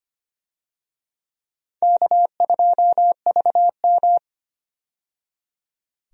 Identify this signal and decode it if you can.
Morse code, heard as a keyed tone, about 25 wpm, 700 Hz. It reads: K2VM
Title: USA Call Signs - 4 Characters - 25wpm